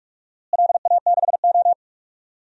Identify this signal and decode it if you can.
Morse code, heard as a keyed tone, about 45 wpm, 700 Hz. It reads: LA6O
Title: DXCC-100 Call Signs - 4 Characters - Repeated 3x - 45wpm